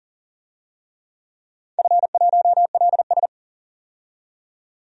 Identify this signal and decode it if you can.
Morse code, heard as a keyed tone, about 39 wpm, 700 Hz. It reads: F1LS
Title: DXCC-100 Call Signs - 4 Characters - 40wpm